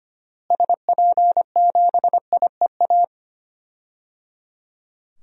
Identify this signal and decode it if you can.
Morse code, heard as a keyed tone, about 25 wpm, 700 Hz. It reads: SP7IEA
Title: DXCC-100 Call Signs - 6 Characters - Repeated 3x - 25wpm